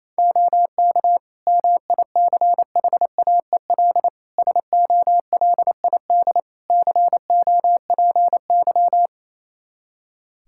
Morse code, heard as a keyed tone, about 28 wpm, 700 Hz. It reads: OK MICHAEL SOLID COPY
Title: K1USN SST QSO Practice 28wpm